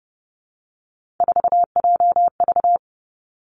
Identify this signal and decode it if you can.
Morse code, heard as a keyed tone, about 32 wpm, 700 Hz. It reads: 4JV
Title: Three Challenging Letter-Number Combinations 30wpm